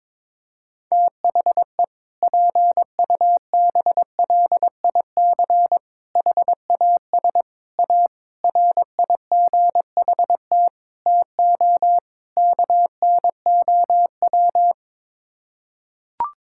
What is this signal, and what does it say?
Morse code, heard as a keyed tone, about 22 wpm, 700 Hz. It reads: THE PUBLIC HAS A RIGHT TO KNOW
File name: practice / 22wpm / Sets of 7 Words - 22wpm